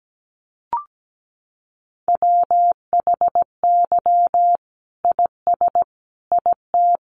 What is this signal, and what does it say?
Morse code, heard as a keyed tone, about 17 wpm, 700 Hz. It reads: WHY IS IT
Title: Sentences from Top 1400 Words 17wpm